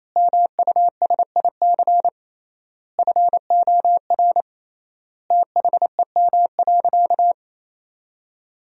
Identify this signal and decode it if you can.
Morse code, heard as a keyed tone, about 28 wpm, 700 Hz. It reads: MUSIC FOR THEM.
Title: 3x Word-Spacing - Sentences from Top 300 Words - 28wpm